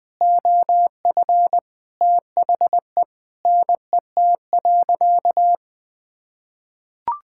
Morse code, heard as a keyed tone, about 20 wpm, 700 Hz. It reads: OF THE NET.
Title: Sentences with 9-Word Phrases - Spoken Beginning - 20wpm